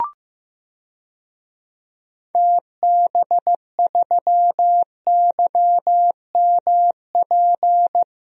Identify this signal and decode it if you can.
Morse code, heard as a keyed tone, about 15 wpm, 700 Hz. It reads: TB3YMP
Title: DXCC-100 Call Signs - 6 Characters - 15wpm